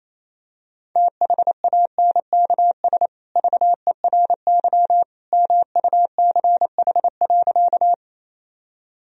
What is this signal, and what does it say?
Morse code, heard as a keyed tone, about 28 wpm, 700 Hz. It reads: THANKS VERY MUCH.
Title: Sentences from Top 1000 Words - Review Part 3 - 28wpm